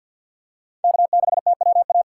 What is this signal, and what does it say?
Morse code, heard as a keyed tone, about 52 wpm, 700 Hz. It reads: K6TWA
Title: USA Call Signs - 2x Character Spacing - 50wpm